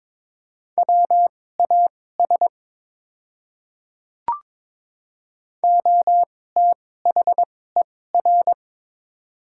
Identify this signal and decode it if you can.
Morse code, heard as a keyed tone, about 22 wpm, spaced as for 16 wpm, 700 Hz. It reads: WAS OTHER
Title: 2x Character Spacing - Top 100 Words - 22wpm